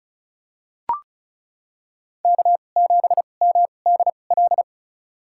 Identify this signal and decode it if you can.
Morse code, heard as a keyed tone, about 36 wpm, spaced as for 25 wpm, 700 Hz. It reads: K7MDL
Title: USA Call Signs - 5 Characters - 2x Character Spacing - 35wpm